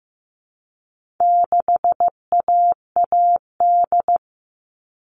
Morse code, heard as a keyed tone, about 15 wpm, 700 Hz. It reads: 6AAD
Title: All Cut Numbers - 4-Digit 15wpm